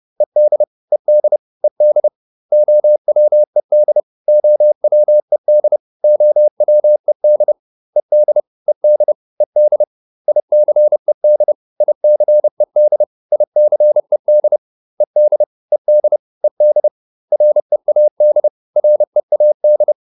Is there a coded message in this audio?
Yes: Morse code from 0.2 to 19.9 s, heard as a keyed tone, about 30 wpm, 600 Hz. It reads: ED ED ED OWED OWED OWED ED ED ED ICED ICED ICED ED ED ED READ READ